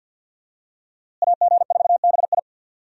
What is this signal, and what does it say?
Morse code, heard as a keyed tone, about 52 wpm, 690 Hz. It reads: AG4BI
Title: USA Call Signs - 5 Characters - Repeated 3x - 50wpm